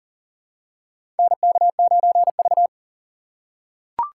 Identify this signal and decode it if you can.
Morse code, heard as a keyed tone, about 40 wpm, 700 Hz. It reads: NK9V
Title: USA Call Signs - 4 Characters - 4x Character Spacing - 40wpm